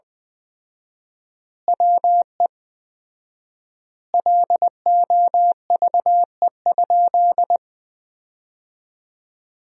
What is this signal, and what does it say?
Morse code, heard as a keyed tone, about 21 wpm, 700 Hz. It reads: WE LOVE?
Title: 4x Word-Spacing - Sentences from Top 400 Words - 20wpm